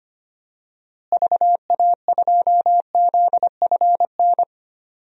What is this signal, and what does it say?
Morse code, heard as a keyed tone, about 25 wpm, 700 Hz. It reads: VA2ZFN
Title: DXCC-100 Call Signs - 6 Characters - Repeated 3x - 25wpm